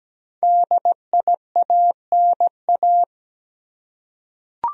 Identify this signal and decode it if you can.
Morse code, heard as a keyed tone, about 17 wpm, 700 Hz. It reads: DIANA
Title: USA Common Names - 17wpm